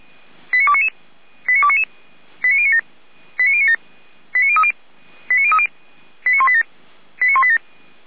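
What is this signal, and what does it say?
Five-tone selective call (ZVEI), heard as quick runs of five tones, tones of about 70 ms. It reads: 89299 89299 89097 89097 89030 89030 89187 89187
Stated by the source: ZVEI five-tone selective call 89294, received by SDR.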